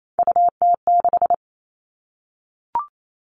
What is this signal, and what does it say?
Morse code, heard as a keyed tone, about 28 wpm, 700 Hz. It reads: UT6
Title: All Cut Numbers - 3-Digit 28wpm